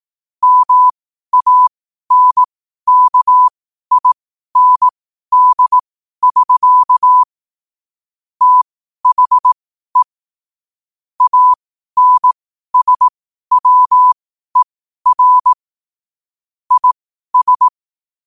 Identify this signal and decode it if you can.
Morse code, heard as a keyed tone, about 18 wpm, spaced as for 12 wpm, 1000 Hz. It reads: MANKIND<SK> THE ANSWER IS
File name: qotdfiles / 12wpm